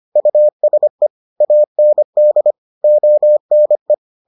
Morse code, heard as a keyed tone, about 25 wpm, 600 Hz. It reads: USE AND ONE